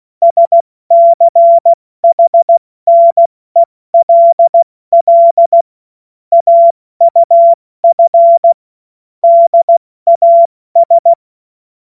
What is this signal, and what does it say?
Morse code, heard as a keyed tone, about 16 wpm, 680 Hz. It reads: SCHNELL AUF DAS
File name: Morsekurs / Witze / Witz 21. Dezember 2021, 70 F80